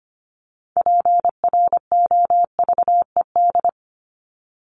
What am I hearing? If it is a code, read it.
Morse code, heard as a keyed tone, about 25 wpm, 700 Hz. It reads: PROVED
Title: ED Suffix - Top 100 Words with Base Words 25wpm